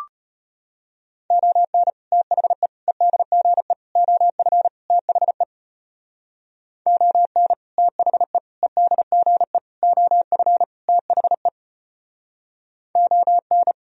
Morse code, heard as a keyed tone, about 33 wpm, 700 Hz. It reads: ON THE EDGE OF THE ON THE EDGE OF THE ON